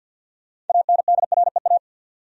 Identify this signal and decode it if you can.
Morse code, heard as a keyed tone, about 50 wpm, 700 Hz. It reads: ANDREA
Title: USA Common Names - 50wpm